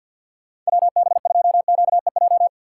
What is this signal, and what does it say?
Morse code, heard as a keyed tone, about 66 wpm, spaced as for 58 wpm, 700 Hz. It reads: WB2QEJ
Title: Speed Racing - USA 2x3 Call Signs 50wpm